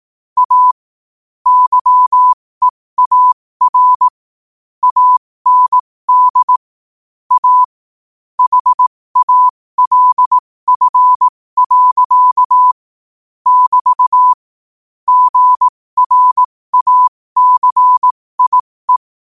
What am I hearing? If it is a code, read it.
Morse code, heard as a keyed tone, about 18 wpm, spaced as for 15 wpm, 1000 Hz. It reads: A YEAR AND A HALF. <BT> GRACIE
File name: qotdfiles / 15wpm